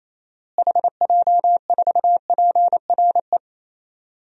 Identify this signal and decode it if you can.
Morse code, heard as a keyed tone, about 28 wpm, 700 Hz. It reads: HJ4PRE
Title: DXCC-100 Call Signs - 6 Characters - Repeated 2x - 28wpm